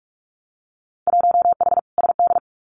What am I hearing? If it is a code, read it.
Morse code, heard as a keyed tone, about 46 wpm, 700 Hz. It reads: 1H SD
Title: Winter Field Day Exchanges 45wpm